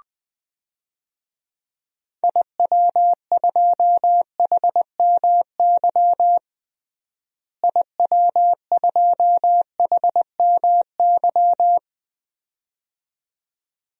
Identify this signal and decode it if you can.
Morse code, heard as a keyed tone, about 20 wpm, 700 Hz. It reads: IW2HMY IW2HMY
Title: DXCC-100 Call Signs - 6 Characters - Repeated 2x - 20wpm